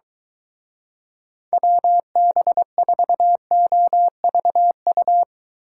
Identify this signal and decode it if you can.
Morse code, heard as a keyed tone, about 23 wpm, 700 Hz. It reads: WB4OVU